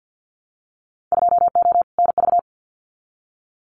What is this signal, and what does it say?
Morse code, heard as a keyed tone, about 50 wpm, 700 Hz. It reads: 2O NV
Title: Winter Field Day Exchanges 50wpm